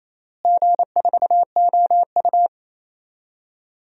Morse code, heard as a keyed tone, about 27 wpm, 700 Hz. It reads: G4OU
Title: DXCC-100 Call Signs - 4 Characters - Repeated 3x - 28wpm